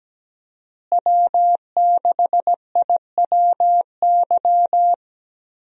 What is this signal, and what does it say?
Morse code, heard as a keyed tone, about 17 wpm, 700 Hz. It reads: W6IWY
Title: USA Call Signs - 5 Characters - Repeated 2x - 17wpm